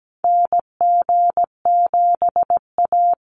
Morse code, heard as a keyed tone, about 17 wpm, 700 Hz. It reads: NG7A
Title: NA SOTA Activator Callsigns 17wpm